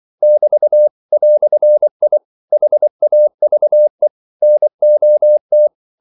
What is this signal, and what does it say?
Morse code, heard as a keyed tone, about 24 wpm, 600 Hz. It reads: <BT> "I HAVE NOT
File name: CW / Practice / EnglishQuotations0224